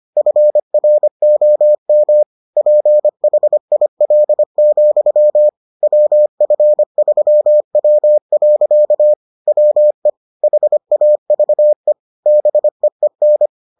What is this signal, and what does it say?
Morse code, heard as a keyed tone, about 25 wpm, 600 Hz. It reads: FROM PHIL, WF3W. WE HAVE BEEN